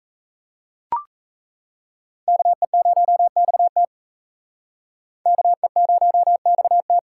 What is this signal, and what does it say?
Morse code, heard as a keyed tone, about 34 wpm, 700 Hz. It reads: KE0XT KE0XT
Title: Speed Racing - USA 2x2 Call Signs 28wpm